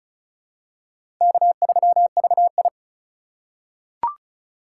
Morse code, heard as a keyed tone, about 34 wpm, 700 Hz. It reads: K3VI
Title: USA Call Signs - 4 Characters - 3x Character Spacing - 35wpm